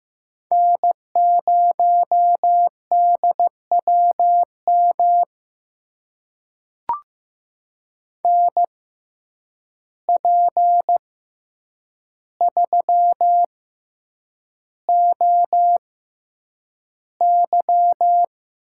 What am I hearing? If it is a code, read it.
Morse code, heard as a keyed tone, about 15 wpm, spaced as for 8 wpm, 700 Hz. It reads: N0DWM NP3OY
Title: USA Call Signs - 5 Characters - 6x Character Spacing - 15wpm